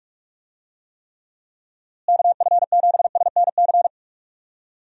Morse code, heard as a keyed tone, about 46 wpm, 700 Hz. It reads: KF7SNC
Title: USA Call Signs - 6 Characters - Repeated 2x - 45wpm